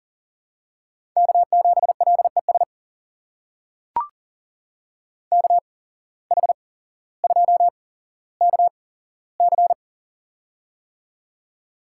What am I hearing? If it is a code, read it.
Morse code, heard as a keyed tone, about 38 wpm, spaced as for 17 wpm, 700 Hz. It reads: K7LES KH2KC
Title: USA Call Signs - 5 Characters - 8x Character Spacing - 40wpm